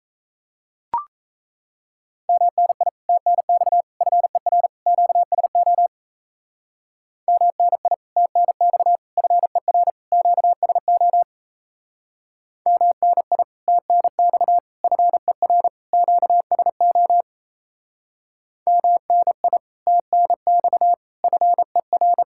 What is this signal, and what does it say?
Morse code, heard as a keyed tone, about 38 wpm, 700 Hz. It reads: MNI TNX FER QSO MNI TNX FER QSO MNI TNX FER QSO MNI TNX FER